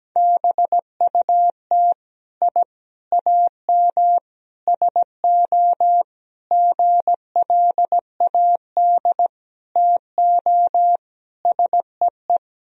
Morse code, heard as a keyed tone, about 17 wpm, 700 Hz. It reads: BUT I AM SO GLAD TO SEE
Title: Sentences from Top 1000 Words - Review Part 2 - 17wpm